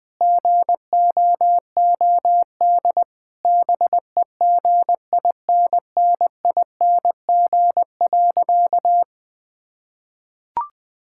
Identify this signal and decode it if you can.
Morse code, heard as a keyed tone, about 20 wpm, 700 Hz. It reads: GOOD BEGINNING.
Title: Sentences from Top 1000 Words - Review Part 5 - 20wpm